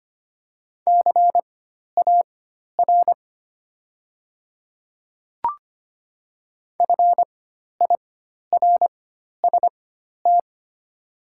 Morse code, heard as a keyed tone, about 25 wpm, spaced as for 12 wpm, 700 Hz. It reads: CAR FIRST
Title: 4x Character Spacing - Top 300 Words - 25wpm